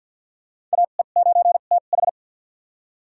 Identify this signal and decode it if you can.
Morse code, heard as a keyed tone, about 51 wpm, spaced as for 36 wpm, 700 Hz. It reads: AE9TH
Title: USA Call Signs - 2x Character Spacing - 50wpm